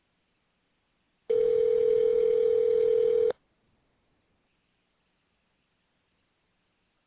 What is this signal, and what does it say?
Ringback tone (North America), 440 + 480 Hz.